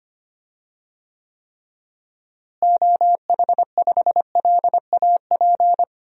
Morse code, heard as a keyed tone, about 25 wpm, 700 Hz. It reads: OH5LAP
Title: DXCC-100 Call Signs - 6 Characters - Repeated 3x - 25wpm